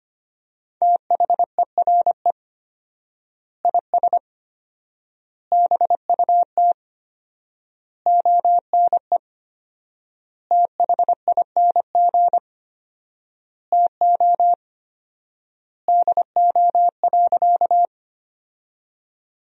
Morse code, heard as a keyed tone, about 25 wpm, 700 Hz. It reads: THERE IS BUT ONE THING TO DO.
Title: 4x Word-Spacing - Sentences from Top 100 Words - 25wpm